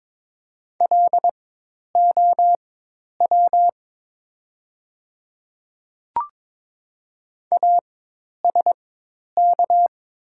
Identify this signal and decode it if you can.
Morse code, heard as a keyed tone, about 22 wpm, spaced as for 10 wpm, 700 Hz. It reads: LOW ASK
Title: 4x Character Spacing - Top 200 Words - 22wpm